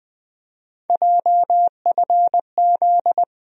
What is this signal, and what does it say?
Morse code, heard as a keyed tone, about 20 wpm, 700 Hz. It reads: JFZ
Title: USA Call Sign Postfixes - 3 Letters 20wpm